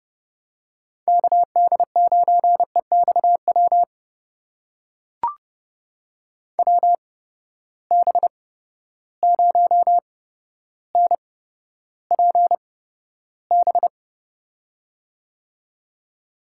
Morse code, heard as a keyed tone, about 29 wpm, spaced as for 13 wpm, 700 Hz. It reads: KD9EXW WB0NPB
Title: USA Call Signs - 6 Characters - 8x Character Spacing - 30wpm